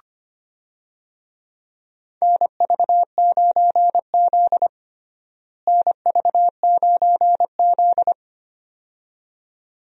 Morse code, heard as a keyed tone, about 25 wpm, 700 Hz. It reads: NV9Z NV9Z
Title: USA Call Signs - 4 Characters - Repeated 2x - 25wpm